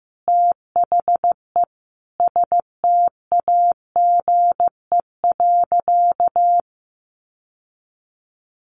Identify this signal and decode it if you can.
Morse code, heard as a keyed tone, about 15 wpm, 700 Hz. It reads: THE STAGE.
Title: Sentences from Top 1200 Words 15wpm